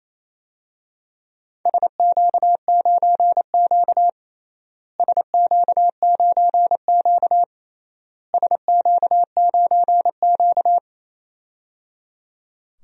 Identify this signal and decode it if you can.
Morse code, heard as a keyed tone, about 28 wpm, 700 Hz. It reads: SQ9Q SQ9Q SQ9Q
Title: DXCC-100 Call Signs - 4 Characters - Repeated 3x - 28wpm